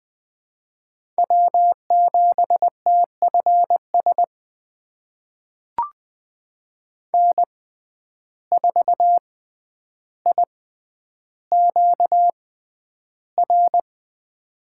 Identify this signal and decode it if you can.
Morse code, heard as a keyed tone, about 20 wpm, spaced as for 10 wpm, 700 Hz. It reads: W7TFS N4IQR
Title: USA Call Signs - 6x Character Spacing - 20wpm